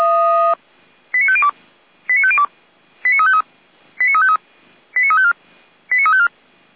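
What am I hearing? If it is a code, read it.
Five-tone selective call (ZVEI), heard as quick runs of five tones, tones of about 70 ms. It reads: 89502 89502 89353 89353 89354 89354
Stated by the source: ZVEI five-tone selective call received on an SDR, recorded by the feuerwehr-orlamuende.org fire brigade server.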